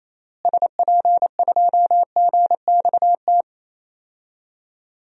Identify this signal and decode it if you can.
Morse code, heard as a keyed tone, about 28 wpm, 700 Hz. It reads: SP2GXT
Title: DXCC-100 Call Signs - 6 Characters - Repeated 2x - 28wpm